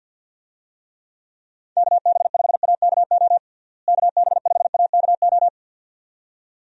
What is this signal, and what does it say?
Morse code, heard as a keyed tone, about 50 wpm, 700 Hz. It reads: KB5AKO KB5AKO
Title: USA Call Signs - 6 Characters - Repeated 2x - 50wpm